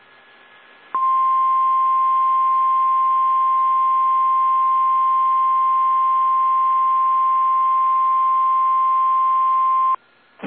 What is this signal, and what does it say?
Emergency Alert System (SAME) tones (the 1050 Hz attention tone). It is heard from 0.9 to 10.0 s.
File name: eas / KSDZKDJL / KSDZKDJL